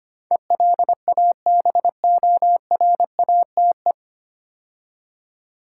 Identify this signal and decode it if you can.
Morse code, heard as a keyed tone, about 25 wpm, 700 Hz. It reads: ELABORATE
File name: practice / 25wpm / 9 Letter Words - First 3 Letters Given - 25wpm